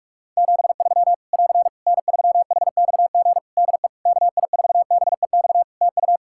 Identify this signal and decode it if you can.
Morse code, heard as a keyed tone, about 46 wpm, 700 Hz. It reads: 73 <AR> N3HXG DE KI4BEX TU
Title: K1USN SST QSO Practice 45wpm